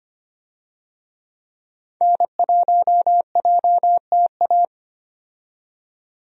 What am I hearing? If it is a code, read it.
Morse code, heard as a keyed tone, about 25 wpm, 700 Hz. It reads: N1JTA
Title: USA Call Signs - 5 Characters - 25wpm